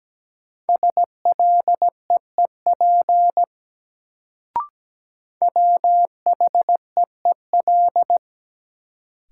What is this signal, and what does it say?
Morse code, heard as a keyed tone, about 17 wpm, 700 Hz. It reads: SLEEP WHEEL
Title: Sentences from Top 500 Words - New Word Drill - 17wpm